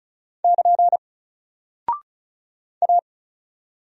Morse code, heard as a keyed tone, about 36 wpm, 700 Hz. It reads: <KN> A
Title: Single Character (Letter:Number:Punctuation:Prosign) Alt 35wpm